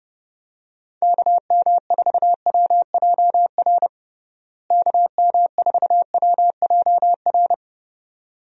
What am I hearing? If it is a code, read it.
Morse code, heard as a keyed tone, about 30 wpm, 700 Hz. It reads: KM4WJR KM4WJR
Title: USA Call Signs - 6 Characters - Repeated 2x - 30wpm